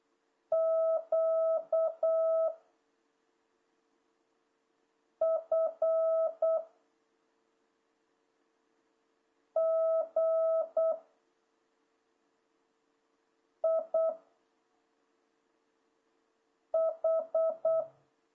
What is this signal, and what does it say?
Morse code, heard as a keyed tone, about 8 wpm, 660 Hz. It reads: Q F G I H
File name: jh8cbh / 2026cw